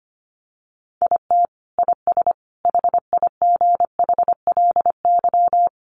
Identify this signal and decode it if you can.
Morse code, heard as a keyed tone, about 25 wpm, 700 Hz. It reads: IT IS HIGHLY
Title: Sentences from Top 1300 Words 25wpm